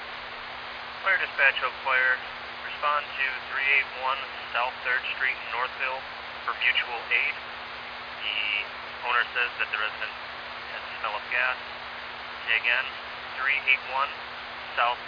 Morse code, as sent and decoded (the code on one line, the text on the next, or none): none